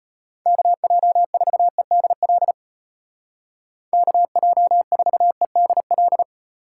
Morse code, heard as a keyed tone, about 34 wpm, 700 Hz. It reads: KJ4EDL KJ4EDL